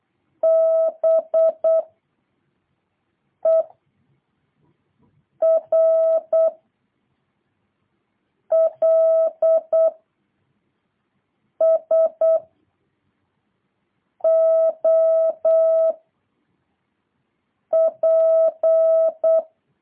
Morse code, heard as a keyed tone, about 8 wpm, 650 Hz. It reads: B E R L S O P